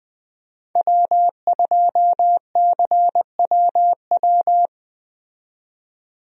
Morse code, heard as a keyed tone, about 20 wpm, 700 Hz. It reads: W2CWW
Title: USA Call Signs - 5 Characters - 20wpm